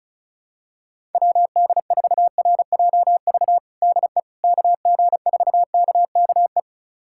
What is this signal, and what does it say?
Morse code, heard as a keyed tone, about 36 wpm, 700 Hz. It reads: WD4RJV DE KG4KKE
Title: USA Callsign DE Callsign - 6 Characters - 35wpm